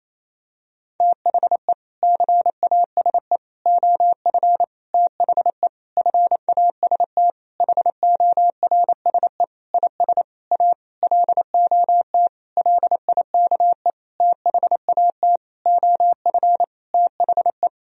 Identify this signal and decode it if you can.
Morse code, heard as a keyed tone, about 28 wpm, 700 Hz. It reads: THE CASE OF THE FAST HORSE IS A LOT LIKE THAT OF THE
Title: Sentences from Top 1000 Words - Review Part 2 - 28wpm